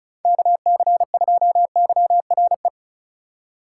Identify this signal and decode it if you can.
Morse code, heard as a keyed tone, about 34 wpm, 700 Hz. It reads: KC2YRE
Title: USA Call Signs - 6 Characters - 3x Character Spacing - 35wpm